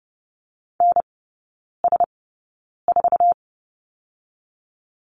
Morse code, heard as a keyed tone, about 31 wpm, spaced as for 9 wpm, 700 Hz. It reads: NS4
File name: Lesson 17e - Letter u - Three Characters 30wpm